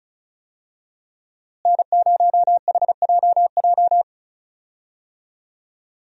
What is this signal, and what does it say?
Morse code, heard as a keyed tone, about 34 wpm, 700 Hz. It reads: N0HJJ
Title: USA Call Signs - 5 Characters - Repeated 2x - 35wpm